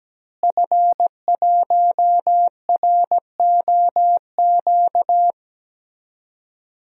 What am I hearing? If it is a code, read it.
Morse code, heard as a keyed tone, about 17 wpm, 700 Hz. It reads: F1ROQ
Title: DXCC-100 Call Signs - 5 Characters - Repeated 3x - 17wpm